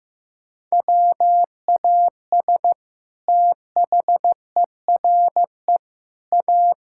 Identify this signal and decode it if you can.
Morse code, heard as a keyed tone, about 15 wpm, 700 Hz. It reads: WAS THERE A
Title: Sentences from Top 1600 Words 15wpm